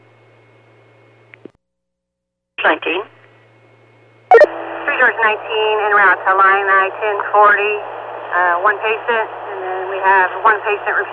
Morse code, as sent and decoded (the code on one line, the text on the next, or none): none